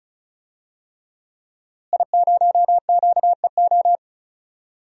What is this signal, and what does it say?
Morse code, heard as a keyed tone, about 36 wpm, 700 Hz. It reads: I0QEO
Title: DXCC-100 Call Signs - 5 Characters - Repeated 3x - 35wpm